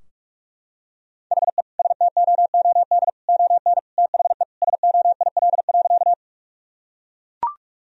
Morse code, heard as a keyed tone, about 41 wpm, 700 Hz. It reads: HE STOOD ON THE SOIL.
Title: Sentences from Top 1000 Words - Review Part 4 - 45wpm